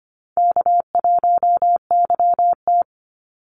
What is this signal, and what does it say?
Morse code, heard as a keyed tone, about 24 wpm, 700 Hz. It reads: K1YT
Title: Easy Call Signs Repeated 3x - 25wpm